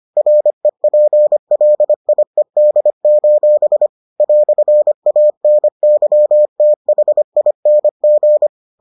Morse code, heard as a keyed tone, about 25 wpm, 600 Hz. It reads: REPLIED: "ANYTHING